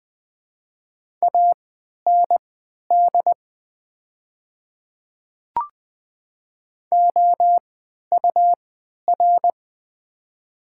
Morse code, heard as a keyed tone, about 21 wpm, spaced as for 11 wpm, 700 Hz. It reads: AND OUR
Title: 3x Character Spacing - Top 200 Words - 20wpm